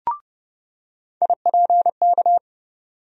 Morse code, heard as a keyed tone, about 29 wpm, 700 Hz. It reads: IPK
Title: USA Call Sign Postfixes - 3 Letters 30wpm